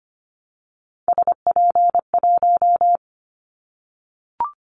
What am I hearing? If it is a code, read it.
Morse code, heard as a keyed tone, about 25 wpm, 700 Hz. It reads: SP1
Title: Callsign Prefixes for Top 25 DXCC - Repeated 3x - 25wpm